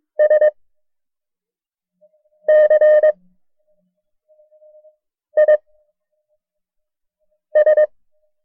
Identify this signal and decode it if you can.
Morse code, heard as a keyed tone, about 22 wpm, 610 Hz. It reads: S C I S
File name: FreeFiles / CW-practice / KB6MT-tape1-side2